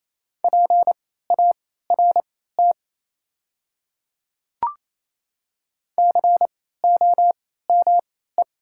Morse code, heard as a keyed tone, about 27 wpm, spaced as for 16 wpm, 700 Hz. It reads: PART COME
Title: 3x Character Spacing - Top 200 Words - 28wpm